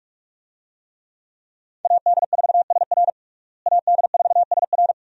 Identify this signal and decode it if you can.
Morse code, heard as a keyed tone, about 46 wpm, 700 Hz. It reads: AD4SR AD4SR
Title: USA Call Signs - 5 Characters - Repeated 2x - 45wpm